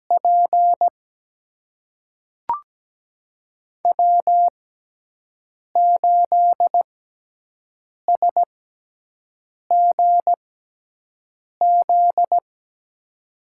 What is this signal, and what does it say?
Morse code, heard as a keyed tone, about 17 wpm, spaced as for 6 wpm, 700 Hz. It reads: P W8SGZ
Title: USA Call Signs - 5 Characters - 6x Character Spacing - 17wpm